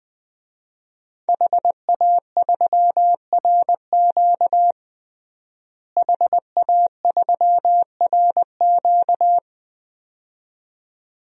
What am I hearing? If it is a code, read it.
Morse code, heard as a keyed tone, about 20 wpm, 700 Hz. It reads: HA3RQ HA3RQ
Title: DXCC-100 Call Signs - 5 Characters - Repeated 2x - 20wpm